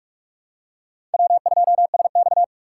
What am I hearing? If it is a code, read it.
Morse code, heard as a keyed tone, about 41 wpm, 700 Hz. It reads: W2SX
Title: USA Call Signs - 3x Character Spacing - 45wpm